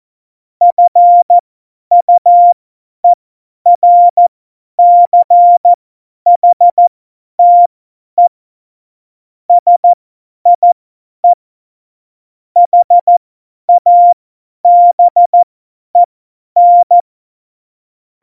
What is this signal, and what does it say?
Morse code, heard as a keyed tone, about 14 wpm, spaced as for 10 wpm, 700 Hz. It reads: FUERCHTE SIE HABEN
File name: Morsekurs / Witze / Witz 06. Februar 2018, 50 F70